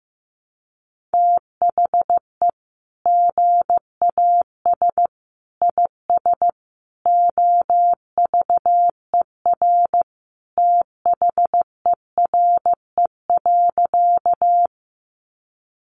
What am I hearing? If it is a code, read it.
Morse code, heard as a keyed tone, about 15 wpm, 700 Hz. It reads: THE GAS IS OVER THERE.
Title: Sentences from Top 1100 Words 15wpm